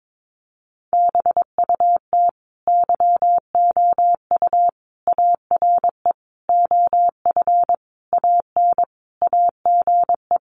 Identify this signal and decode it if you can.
Morse code, heard as a keyed tone, about 22 wpm, 700 Hz. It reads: BUT YOU ARE OF AN AGE
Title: Sentences from Top 1300 Words 22wpm